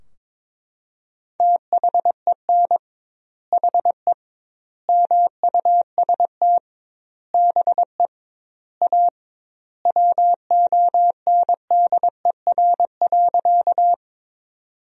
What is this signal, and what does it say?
Morse code, heard as a keyed tone, about 21 wpm, 700 Hz. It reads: THEN HE MUST BE A WONDER.
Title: 2x Word-Spacing - Sentences from Top 500 Words - 22wpm